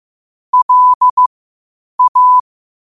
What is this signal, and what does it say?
Morse code, heard as a keyed tone, about 15 wpm, spaced as for 8 wpm, 1000 Hz. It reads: LA
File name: qotdfiles / 8wpm